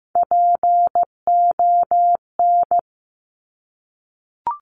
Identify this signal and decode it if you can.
Morse code, heard as a keyed tone, about 15 wpm, 700 Hz. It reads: PON
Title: Top Three Letter Combinations in Words 15wpm